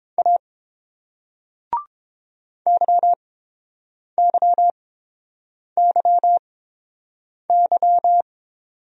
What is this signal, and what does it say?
Morse code, heard as a keyed tone, about 30 wpm, spaced as for 7 wpm, 700 Hz. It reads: A YYYY